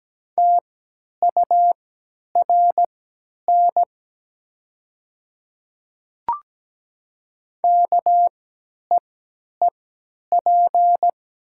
Morse code, heard as a keyed tone, about 17 wpm, spaced as for 10 wpm, 700 Hz. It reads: TURN KEEP
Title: 3x Character Spacing - Top 300 Words - 17wpm